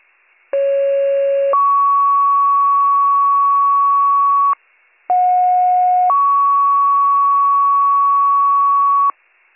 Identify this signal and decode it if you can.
Two-tone page, heard as a one-second tone, then a three-second tone, about 550 then 1120 Hz, about 730 then 1120 Hz. It is heard from 0.5 to 9.1 s.